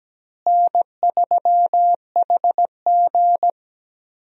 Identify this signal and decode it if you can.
Morse code, heard as a keyed tone, about 17 wpm, 700 Hz. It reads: N3HG
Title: USA Call Signs - 4 Characters - Repeated 3x - 17wpm